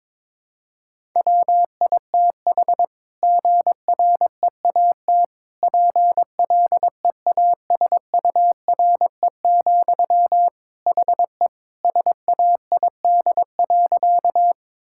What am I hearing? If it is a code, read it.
Morse code, heard as a keyed tone, about 22 wpm, 700 Hz. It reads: WITH GREAT PLEASURE, HE SAID.
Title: Sentences from Top 1600 Words 22wpm